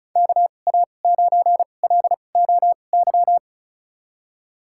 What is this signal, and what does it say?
Morse code, heard as a keyed tone, about 36 wpm, spaced as for 25 wpm, 700 Hz. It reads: KA9LOY
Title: USA Call Signs - 6 Characters - 2x Character Spacing - 35wpm